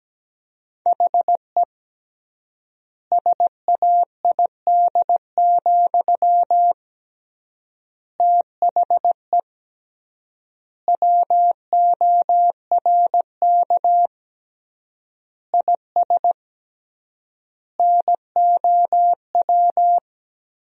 Morse code, heard as a keyed tone, about 17 wpm, 700 Hz. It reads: HE SAID, THE WORK IS NOW
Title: 3x Word-Spacing - Sentences from Top 400 Words - 17wpm